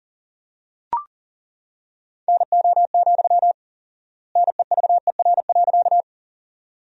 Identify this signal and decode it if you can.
Morse code, heard as a keyed tone, about 40 wpm, 700 Hz. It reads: NO, NEVER.
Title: 4x Word-Spacing - Sentences from Top 300 Words - 40wpm